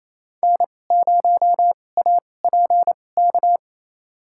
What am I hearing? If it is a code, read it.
Morse code, heard as a keyed tone, about 28 wpm, spaced as for 20 wpm, 700 Hz. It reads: N0APK
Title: USA Call Signs - 5 Characters - 2x Character Spacing - 28wpm